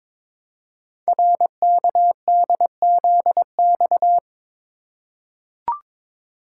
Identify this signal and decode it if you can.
Morse code, heard as a keyed tone, about 22 wpm, 700 Hz. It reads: RKDZX
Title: Code Groups - 5 Letters - 22wpm